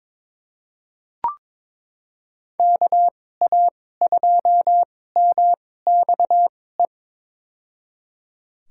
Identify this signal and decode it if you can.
Morse code, heard as a keyed tone, about 21 wpm, spaced as for 16 wpm, 700 Hz. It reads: KA2MXE
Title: USA Call Signs - 6 Characters - 2x Character Spacing - 22wpm